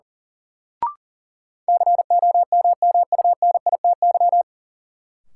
Morse code, heard as a keyed tone, about 37 wpm, 700 Hz. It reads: COMMUNITY
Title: Sentences from Top 1000 Words - Review Part 4 - New Word Drill - 40wpm